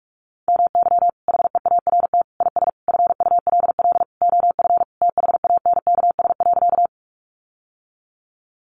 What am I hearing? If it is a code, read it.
Morse code, heard as a keyed tone, about 44 wpm, 700 Hz. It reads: MY HEART IS FULL OF THANKS.
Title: Sentences from Top 1200 Words 45wpm